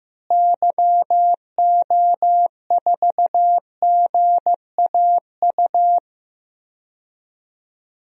Morse code, heard as a keyed tone, about 15 wpm, 700 Hz. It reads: YO4GAU
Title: DXCC-100 Call Signs - 6 Characters - Repeated 3x - 15wpm